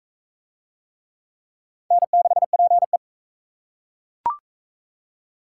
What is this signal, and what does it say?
Morse code, heard as a keyed tone, about 38 wpm, 700 Hz. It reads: N6PE